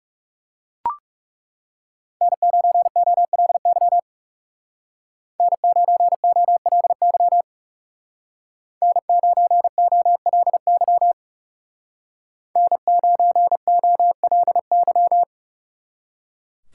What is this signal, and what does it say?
Morse code, heard as a keyed tone, about 35 wpm, 700 Hz. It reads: N9OLY N9OLY N9OLY N9OLY
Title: Speed Racing - USA Call Signs 30wpm